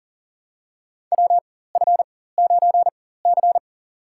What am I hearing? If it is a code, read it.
Morse code, heard as a keyed tone, about 40 wpm, spaced as for 19 wpm, 700 Hz. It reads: WF9C
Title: USA Call Signs - 4 Characters - 4x Character Spacing - 40wpm